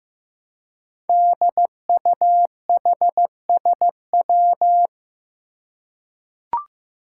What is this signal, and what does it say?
Morse code, heard as a keyed tone, about 15 wpm, 700 Hz. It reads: DUHSW
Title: Five Challenging Letter-Number Combinations 15wpm